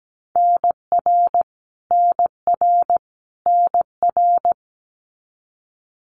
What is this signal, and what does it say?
Morse code, heard as a keyed tone, about 17 wpm, 700 Hz. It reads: NR NR NR
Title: Abbreviations Repeated 3x - 17wpm